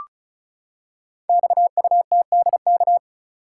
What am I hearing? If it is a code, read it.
Morse code, heard as a keyed tone, about 36 wpm, 700 Hz. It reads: XUTDK
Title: Code Groups - 5 Letters - 35wpm